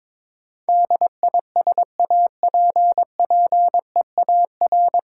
Morse code, heard as a keyed tone, about 21 wpm, 700 Hz. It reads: DISAPPEAR
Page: Morse Code Ninja practice 22wpm